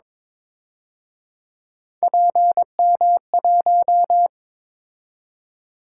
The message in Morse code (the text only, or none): PM1